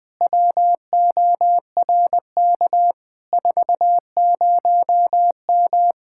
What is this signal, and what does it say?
Morse code, heard as a keyed tone, about 20 wpm, 700 Hz. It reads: WORK 40M
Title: Speed-Racing Common QSO Phrases Encore - 20wpm